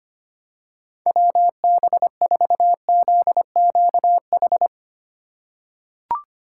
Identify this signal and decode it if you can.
Morse code, heard as a keyed tone, about 25 wpm, 700 Hz. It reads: WB4ZQH
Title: USA Call Signs - 3x Character Spacing - 25wpm